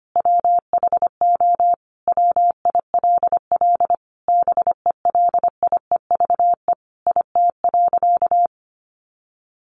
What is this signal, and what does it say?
Morse code, heard as a keyed tone, about 25 wpm, 700 Hz. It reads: WHO WILL BELIEVE IT.
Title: Sentences from Top 1000 Words 25wpm